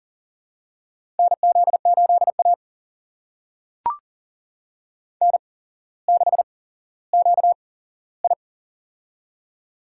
Morse code, heard as a keyed tone, about 39 wpm, spaced as for 17 wpm, 700 Hz. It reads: NZ8A N6QI
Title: USA Call Signs - 4 Characters - 8x Character Spacing - 40wpm